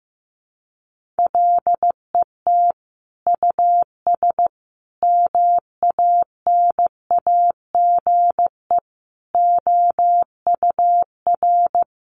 Morse code, heard as a keyed tone, about 15 wpm, 700 Hz. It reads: LET US MANAGE OUR
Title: Sentences from Top 1400 Words 15wpm